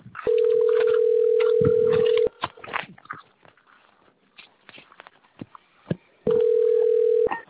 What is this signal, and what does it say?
Ringback tone (North America), 440 + 480 Hz.